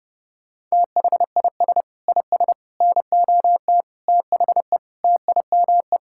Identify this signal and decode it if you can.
Morse code, heard as a keyed tone, about 30 wpm, 700 Hz. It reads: THIS IS NOT THE TIME
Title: Sentences from Top 1000 Words - Review Part 5 - 30wpm